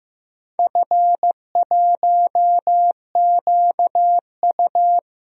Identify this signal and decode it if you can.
Morse code, heard as a keyed tone, about 15 wpm, 700 Hz. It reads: F1QU
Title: DXCC-100 Call Signs - 4 Characters - Repeated 2x - 15wpm